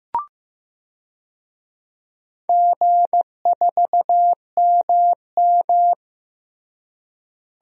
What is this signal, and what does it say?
Morse code, heard as a keyed tone, about 15 wpm, 700 Hz. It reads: G4MM
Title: DXCC-100 Call Signs - 4 Characters - Repeated 2x - 15wpm